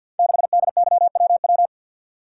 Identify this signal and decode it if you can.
Morse code, heard as a keyed tone, about 52 wpm, 700 Hz. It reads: 6DYWW
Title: Code Groups - 5 Letter:Numbers - 50wpm